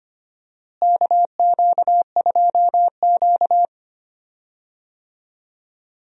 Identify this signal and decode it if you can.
Morse code, heard as a keyed tone, about 25 wpm, 700 Hz. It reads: KQ2Q
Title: USA Call Signs - 4 Characters - Repeated 2x - 25wpm